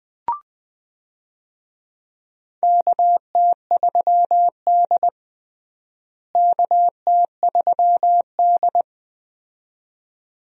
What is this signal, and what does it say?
Morse code, heard as a keyed tone, about 20 wpm, 700 Hz. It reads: KT3D KT3D
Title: USA Call Signs - 4 Characters - Repeated 2x - 20wpm